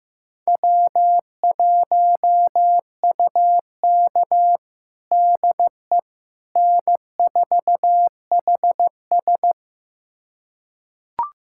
Morse code, heard as a keyed tone, about 15 wpm, 700 Hz. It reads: W1UK DE N4HS
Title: USA Callsign DE Callsign - 4 Characters - 15wpm